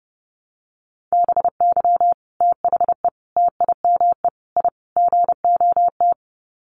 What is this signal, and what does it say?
Morse code, heard as a keyed tone, about 30 wpm, 700 Hz. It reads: BY THE TIME I GOT
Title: Sets of 5 Words Encore2 30wpm